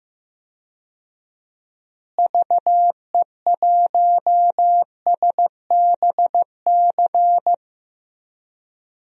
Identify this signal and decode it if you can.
Morse code, heard as a keyed tone, about 15 wpm, 700 Hz. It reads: VE1SBC
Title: DXCC-100 Call Signs - 6 Characters - Repeated 2x - 15wpm